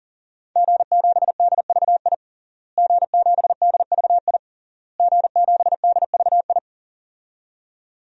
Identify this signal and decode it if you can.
Morse code, heard as a keyed tone, about 38 wpm, 700 Hz. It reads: G7DVI G7DVI G7DVI
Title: DXCC-100 Call Signs - 5 Characters - Repeated 3x - 40wpm